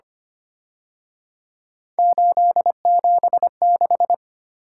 Morse code, heard as a keyed tone, about 29 wpm, 700 Hz. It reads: 876
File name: Speed-Racing - Three Letter-Number Combinations - 25wpm